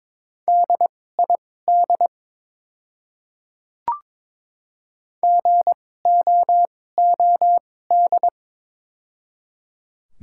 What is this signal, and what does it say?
Morse code, heard as a keyed tone, about 22 wpm, spaced as for 16 wpm, 700 Hz. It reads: DID GOOD
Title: 2x Character Spacing - Top 300 Words - 22wpm